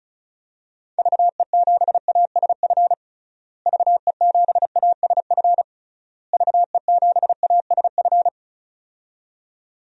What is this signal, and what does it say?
Morse code, heard as a keyed tone, about 36 wpm, 700 Hz. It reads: VE7ASF VE7ASF VE7ASF
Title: DXCC-100 Call Signs - 6 Characters - Repeated 3x - 35wpm